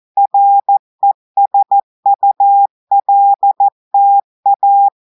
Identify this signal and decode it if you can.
Morse code, heard as a keyed tone, about 14 wpm, 800 Hz. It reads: RESULTA
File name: audios / portugais / portugais texte en clair 1 vitesse 14 mots minutes